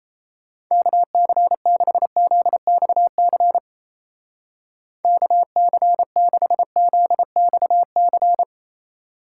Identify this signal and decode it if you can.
Morse code, heard as a keyed tone, about 35 wpm, 700 Hz. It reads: KC6ZXC KC6ZXC